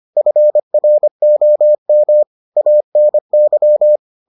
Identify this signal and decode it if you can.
Morse code, heard as a keyed tone, about 25 wpm, 600 Hz. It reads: FROM ANY